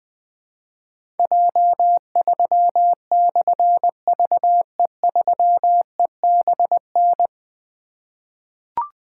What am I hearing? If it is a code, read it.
Morse code, heard as a keyed tone, about 20 wpm, 700 Hz. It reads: J3/VE3EBN
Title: Difficult Call Signs 20wpm